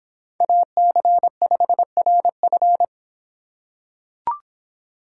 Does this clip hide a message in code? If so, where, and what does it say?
Yes: Morse code from 0.4 to 2.8 s, heard as a keyed tone, about 25 wpm, 700 Hz. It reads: AC5RF